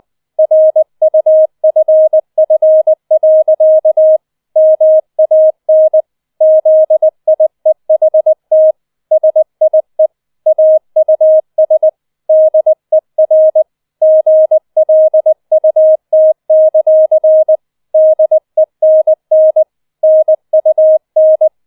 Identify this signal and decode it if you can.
Morse code, heard as a keyed tone, about 20 wpm, 630 Hz. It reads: RUFF. MAN ZIEHT SIE AUS DER GLUT; DENN NUN